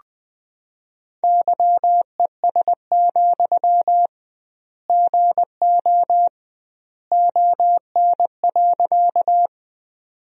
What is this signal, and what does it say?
Morse code, heard as a keyed tone, about 21 wpm, 700 Hz. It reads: YES, GO ON.
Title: 2x Word-Spacing - Sentences from Top 500 Words - 20wpm